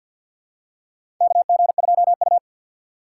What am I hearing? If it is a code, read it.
Morse code, heard as a keyed tone, about 50 wpm, 700 Hz. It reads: KG2U
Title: USA Call Signs - 4 Characters - 3x Character Spacing - 50wpm